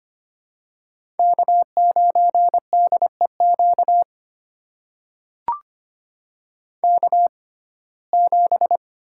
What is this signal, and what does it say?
Morse code, heard as a keyed tone, about 25 wpm, spaced as for 18 wpm, 700 Hz. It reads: K9DEQ K7
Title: USA Call Signs - 5 Characters - 6x Character Spacing - 25wpm